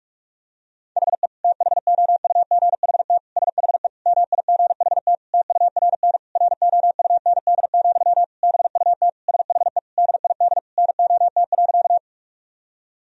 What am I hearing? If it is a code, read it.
Morse code, heard as a keyed tone, about 41 wpm, 700 Hz. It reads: HE THOUGHT SHE MIGHT TURN ROUND, BUT SHE DID NOT.
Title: Sentences from Top 1000 Words - Review Part 1 - 45wpm